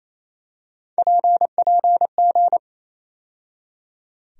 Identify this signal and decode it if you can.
Morse code, heard as a keyed tone, about 28 wpm, 700 Hz. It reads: PPG
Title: USA Call Sign Postfixes - 3 Letters 28wpm